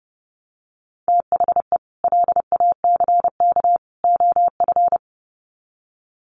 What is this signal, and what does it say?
Morse code, heard as a keyed tone, about 30 wpm, 700 Hz. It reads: THE LACK OF
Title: Sets of 3 Words 30wpm